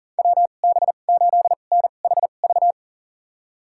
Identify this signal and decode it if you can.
Morse code, heard as a keyed tone, about 40 wpm, spaced as for 29 wpm, 700 Hz. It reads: WB8NHV
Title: USA Call Signs - 6 Characters - 2x Character Spacing - 40wpm